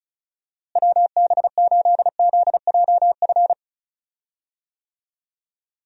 Morse code, heard as a keyed tone, about 34 wpm, 700 Hz. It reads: WB8ZJF
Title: USA Call Signs - 6 Characters - Repeated 3x - 35wpm